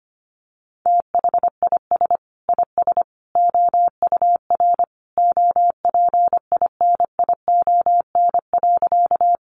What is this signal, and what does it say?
Morse code, heard as a keyed tone, about 25 wpm, 700 Hz. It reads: THIS IS OUR OPINION.
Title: Sentences from Top 1200 Words 25wpm